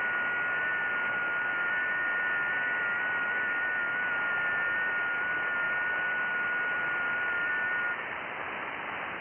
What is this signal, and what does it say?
FT8.
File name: IC-7100